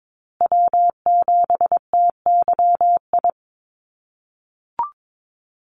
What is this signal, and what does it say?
Morse code, heard as a keyed tone, about 22 wpm, 700 Hz. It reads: W7TYI
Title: USA Call Signs Repeated 2x 22wpm